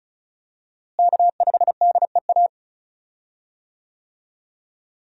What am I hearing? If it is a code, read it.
Morse code, heard as a keyed tone, about 36 wpm, 700 Hz. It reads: K5DEA
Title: USA Call Signs - 5 Characters - Repeated 2x - 35wpm